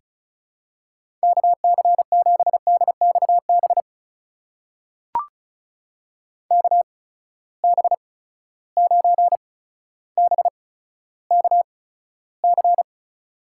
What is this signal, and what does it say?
Morse code, heard as a keyed tone, about 36 wpm, spaced as for 15 wpm, 700 Hz. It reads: KC7DXB KB9BKC
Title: USA Call Signs - 6 Characters - 8x Character Spacing - 35wpm